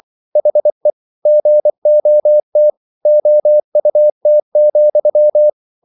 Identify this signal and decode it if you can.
Morse code, heard as a keyed tone, about 24 wpm, 600 Hz. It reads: HE GOT OUT,